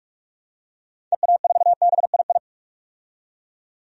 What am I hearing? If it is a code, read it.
Morse code, heard as a keyed tone, about 46 wpm, 700 Hz. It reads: EA4BII
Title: DXCC-100 Call Signs - 6 Characters - 45wpm